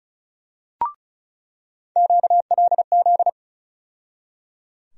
Morse code, heard as a keyed tone, about 36 wpm, 700 Hz. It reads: QLZ
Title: USA Call Sign Postfixes - 3 Letters 35wpm